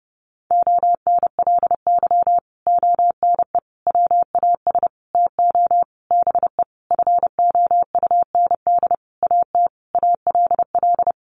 Morse code, heard as a keyed tone, about 30 wpm, 700 Hz. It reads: ONLY ONE WAS TO BE FOUND AT ALL
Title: Sentences from Top 1400 Words 30wpm